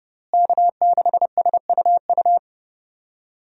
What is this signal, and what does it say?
Morse code, heard as a keyed tone, about 30 wpm, 700 Hz. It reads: K6SUU
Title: USA Call Signs - 5 Characters - Repeated 3x - 30wpm